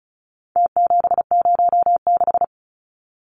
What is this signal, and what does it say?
Morse code, heard as a keyed tone, about 34 wpm, 700 Hz. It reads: T706
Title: Numbers - Contest Style - 4-Digit 35wpm